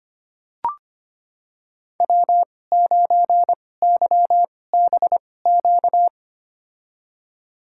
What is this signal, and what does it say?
Morse code, heard as a keyed tone, about 25 wpm, spaced as for 18 wpm, 700 Hz. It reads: W9YBQ
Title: USA Call Signs - 5 Characters - 2x Character Spacing - 25wpm